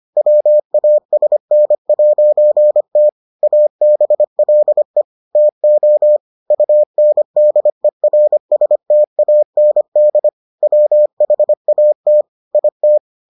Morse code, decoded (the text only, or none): WASN'T ABLE TO UNDERSTAND WHAT IT